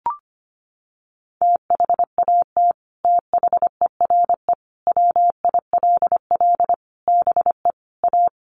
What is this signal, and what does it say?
Morse code, heard as a keyed tone, about 25 wpm, 700 Hz. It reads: THAT THERE WILL BE A
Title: Sets of 5 Words Encore2 25wpm